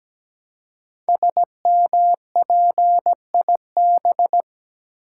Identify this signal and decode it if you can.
Morse code, heard as a keyed tone, about 17 wpm, 700 Hz. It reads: SMPIB
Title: Code Groups - 5 Letters - 17wpm